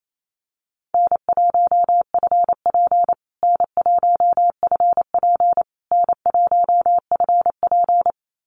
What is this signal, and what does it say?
Morse code, heard as a keyed tone, about 28 wpm, 700 Hz. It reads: N1FP N1FP N1FP
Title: USA Call Signs Repeated 3x 28wpm